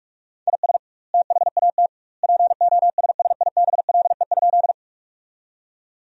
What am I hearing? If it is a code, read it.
Morse code, heard as a keyed tone, about 42 wpm, 700 Hz. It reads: IS THAT POSSIBLE?
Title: 2x Word-Spacing - Sentences from Top 500 Words - 45wpm